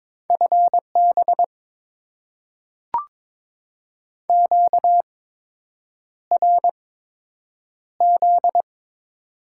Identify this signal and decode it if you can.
Morse code, heard as a keyed tone, about 22 wpm, spaced as for 8 wpm, 700 Hz. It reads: FB QRZ
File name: Top 90 QSO Elements - 8x Character Spacing - 22wpm